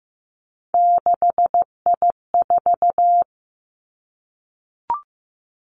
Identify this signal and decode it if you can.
Morse code, heard as a keyed tone, about 15 wpm, 700 Hz. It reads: 6I4
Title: Three Challenging Letter-Number Combinations 15wpm